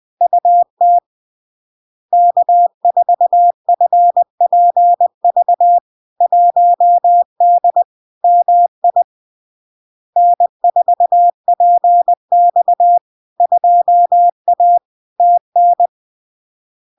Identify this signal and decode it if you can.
Morse code, heard as a keyed tone, about 20 wpm, 700 Hz. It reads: UT K4FPV 1D MI N4PX 2A TN
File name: LARK-FD-2022-Log-in-CW-20wpm-700-Hz